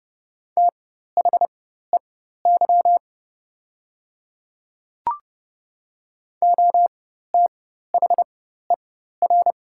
Morse code, heard as a keyed tone, about 30 wpm, spaced as for 14 wpm, 690 Hz. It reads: THEY OTHER
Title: 4x Character Spacing - Top 300 Words - 30wpm